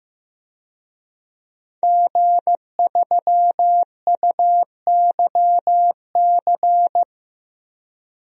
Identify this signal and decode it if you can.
Morse code, heard as a keyed tone, about 15 wpm, 700 Hz. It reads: G3UYC
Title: DXCC-100 Call Signs - 5 Characters - 15wpm